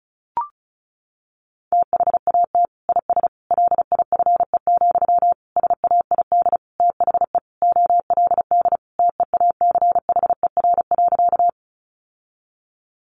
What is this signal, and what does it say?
Morse code, heard as a keyed tone, about 34 wpm, 700 Hz. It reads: THAT IS LIFE, SAID THE OLD TEACHER.
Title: Sentences from Top 900 Words 35wpm